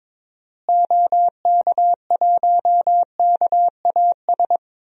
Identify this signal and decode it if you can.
Morse code, heard as a keyed tone, about 22 wpm, 700 Hz. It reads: OK1KAS
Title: DXCC-100 Call Signs - 6 Characters - Repeated 3x - 22wpm